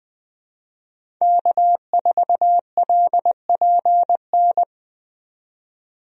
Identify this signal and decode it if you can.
Morse code, heard as a keyed tone, about 21 wpm, 700 Hz. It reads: K4LPN
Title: USA Call Signs - 5 Characters - 20wpm